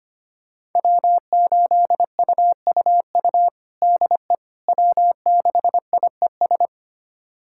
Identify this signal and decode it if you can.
Morse code, heard as a keyed tone, about 25 wpm, 700 Hz. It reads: W8UUU DE W6IES
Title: USA Callsign DE Callsign - 5 Characters - 25wpm